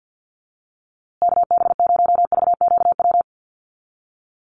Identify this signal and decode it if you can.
Morse code, heard as a keyed tone, about 50 wpm, 700 Hz. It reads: KB0VQW
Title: USA Call Signs Repeated 2x 50wpm